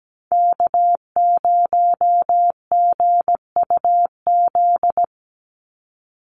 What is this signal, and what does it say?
Morse code, heard as a keyed tone, about 17 wpm, 700 Hz. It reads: K0GUZ
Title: Call Signs Repeated 2x - 17wpm